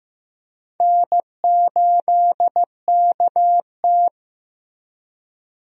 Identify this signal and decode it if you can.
Morse code, heard as a keyed tone, about 15 wpm, 700 Hz. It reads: N8KT
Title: USA Call Signs - 4 Characters - Repeated 2x - 15wpm